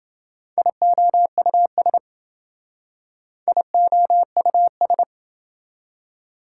Morse code, heard as a keyed tone, about 27 wpm, 700 Hz. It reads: IOUS IOUS